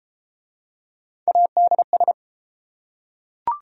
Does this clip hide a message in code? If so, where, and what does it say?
Yes: Morse code from 1.3 to 2.1 s, heard as a keyed tone, about 30 wpm, 700 Hz. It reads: ADS